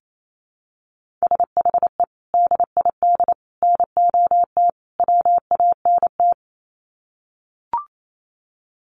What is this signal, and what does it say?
Morse code, heard as a keyed tone, about 28 wpm, 700 Hz. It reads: SHE DID NOT WANT
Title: Sets of 4 Words 28wpm